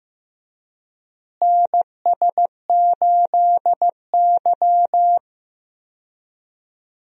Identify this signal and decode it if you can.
Morse code, heard as a keyed tone, about 15 wpm, 700 Hz. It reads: NS8Y
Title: USA Call Signs - 4 Characters - Repeated 2x - 15wpm